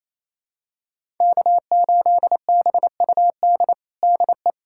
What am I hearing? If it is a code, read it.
Morse code, heard as a keyed tone, about 28 wpm, 700 Hz. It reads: K8BUD DE
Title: USA Callsign DE Callsign - 5 Characters - 28wpm